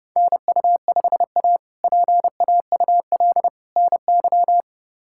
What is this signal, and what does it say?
Morse code, heard as a keyed tone, about 31 wpm, 700 Hz. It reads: NU5A PAUL NY
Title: K1USN SST QSO Practice - 30wpm